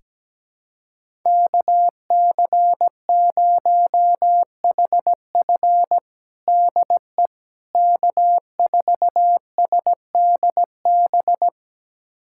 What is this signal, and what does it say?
Morse code, heard as a keyed tone, about 17 wpm, 700 Hz. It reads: KC0HF DE K4SDB
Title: USA Callsign DE Callsign - 5 Characters - 17wpm